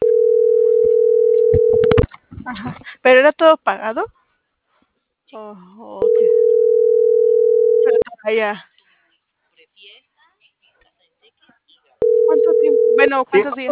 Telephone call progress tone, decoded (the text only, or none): ringback tone